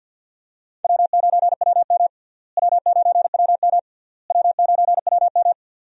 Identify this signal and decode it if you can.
Morse code, heard as a keyed tone, about 50 wpm, 700 Hz. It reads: W9WM W9WM W9WM
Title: USA Call Signs - 4 Characters - Repeated 3x - 50wpm